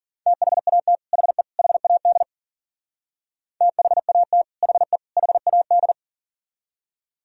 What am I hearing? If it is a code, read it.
Morse code, heard as a keyed tone, about 50 wpm, 700 Hz. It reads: THAT HE HAD THAT HE HAD